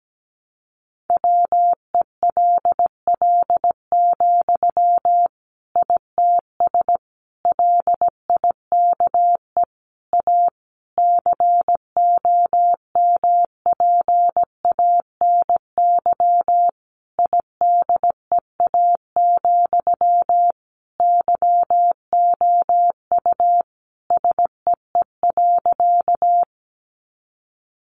Morse code, heard as a keyed tone, about 17 wpm, 700 Hz. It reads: WELL, ITS LIKE A COMPANY IDEA, YOU SEE.
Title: Sentences from Top 600 Words 17wpm